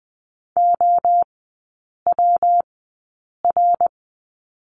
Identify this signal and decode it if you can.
Morse code, heard as a keyed tone, about 20 wpm, spaced as for 8 wpm, 700 Hz. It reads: OWR
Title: 127 Lesson 21e - Number 6 - Three Characters 20wpm